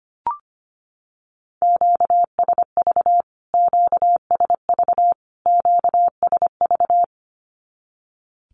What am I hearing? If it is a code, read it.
Morse code, heard as a keyed tone, about 25 wpm, 700 Hz. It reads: QSV QSV QSV
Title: Q-Codes Repeated 3x 25wpm